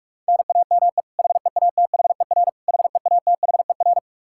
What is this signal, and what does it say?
Morse code, heard as a keyed tone, about 46 wpm, 700 Hz. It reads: NAME HEATHER HEATHER
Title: K1USN SST QSO Practice 45wpm